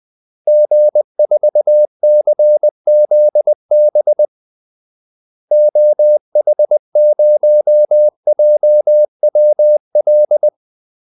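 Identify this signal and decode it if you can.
Morse code, heard as a keyed tone, about 20 wpm, 600 Hz. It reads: G4CZB OH0JWL
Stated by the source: Morse call-sign practice at 20 wpm.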